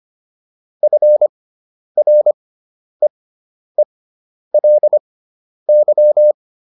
Morse code, heard as a keyed tone, about 25 wpm, spaced as for 10 wpm, 600 Hz. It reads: FREELY